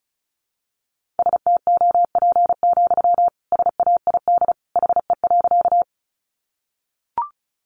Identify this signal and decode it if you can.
Morse code, heard as a keyed tone, about 34 wpm, 700 Hz. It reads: STOP, SAID HE.
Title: Sentences from Top 300 Words 35wpm